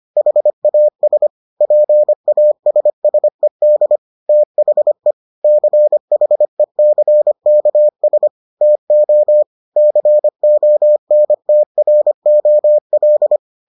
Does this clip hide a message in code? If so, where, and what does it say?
Yes: Morse code from 0.2 to 13.4 s, heard as a keyed tone, about 25 wpm, 600 Hz. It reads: HAS PASSED THE CHECKS TO CONTROL